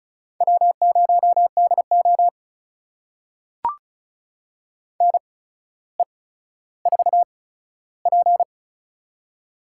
Morse code, heard as a keyed tone, about 36 wpm, spaced as for 15 wpm, 700 Hz. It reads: W0DO NE4P
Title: USA Call Signs - 8x Character Spacing - 35wpm